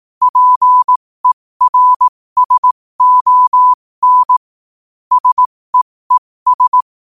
Morse code, heard as a keyed tone, about 18 wpm, spaced as for 15 wpm, 1000 Hz. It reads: PERSON SEES
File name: qotdfiles / 15wpm